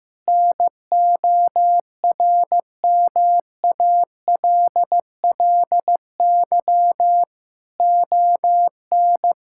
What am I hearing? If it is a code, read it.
Morse code, heard as a keyed tone, about 15 wpm, 700 Hz. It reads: NORMALLY ON
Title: Sentences from Top 1000 Words - Review Part 6 - 15wpm